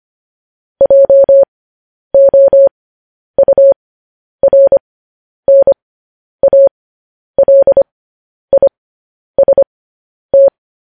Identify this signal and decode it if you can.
Morse code, heard as a keyed tone, about 25 wpm, spaced as for 10 wpm, 550 Hz. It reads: JOURNALIST
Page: news-headlines-in-morse-code-at-10-wpm